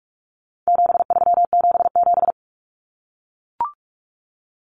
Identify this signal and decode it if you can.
Morse code, heard as a keyed tone, about 44 wpm, 700 Hz. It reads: 7377
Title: Common Cut Numbers - 4-Digit 45wpm